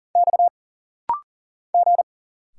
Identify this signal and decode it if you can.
Morse code, heard as a keyed tone, about 40 wpm, 700 Hz. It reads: X G
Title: Single Letter-Number Alt - Rapid-Fire 40wpm